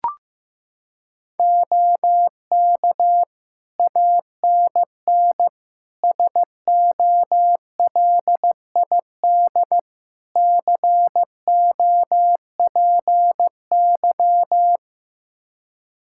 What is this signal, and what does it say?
Morse code, heard as a keyed tone, about 15 wpm, 700 Hz. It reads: OK ANN SOLID COPY
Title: K1USN SST QSO Practice 15wpm